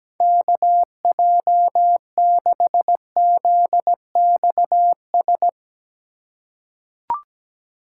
Morse code, heard as a keyed tone, about 17 wpm, 700 Hz. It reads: KJ6ZXS
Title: USA Call Signs - 6 Characters - 4x Character Spacing - 17wpm